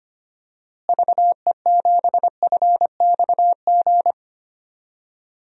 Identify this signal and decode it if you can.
Morse code, heard as a keyed tone, about 25 wpm, 700 Hz. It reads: VE7FXG
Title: DXCC-100 Call Signs - 6 Characters - Repeated 2x - 25wpm